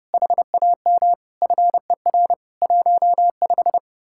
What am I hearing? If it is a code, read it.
Morse code, heard as a keyed tone, about 37 wpm, 700 Hz. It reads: HAM FER 15
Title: Speed-Racing Common QSO Phrases Encore - 30wpm